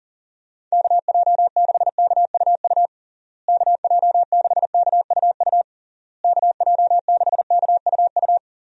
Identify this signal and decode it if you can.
Morse code, heard as a keyed tone, about 40 wpm, 700 Hz. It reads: KJ6KUU KJ6KUU KJ6KUU
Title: USA Call Signs - 6 Characters - Repeated 3x - 40wpm